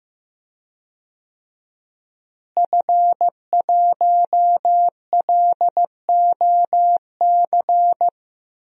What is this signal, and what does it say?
Morse code, heard as a keyed tone, about 15 wpm, 700 Hz. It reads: F1LOC
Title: DXCC-100 Call Signs - 5 Characters - Repeated 3x - 15wpm